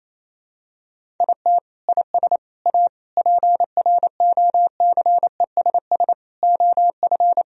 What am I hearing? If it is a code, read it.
Morse code, heard as a keyed tone, about 28 wpm, 700 Hz. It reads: IT IS A PROCESS OF
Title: Sentences from Top 1000 Words - Review Part 4 - 28wpm